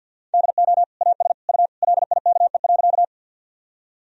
Morse code, heard as a keyed tone, about 50 wpm, 700 Hz. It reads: DO AS U LIKE.
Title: Sentences from Top 100 Words and Abbreviations 50wpm